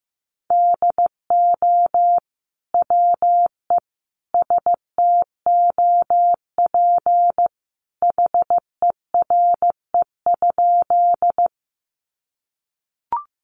Morse code, heard as a keyed tone, about 15 wpm, 700 Hz. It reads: DO WE STOP HERE?
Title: Sentences from Top 300 Words 15wpm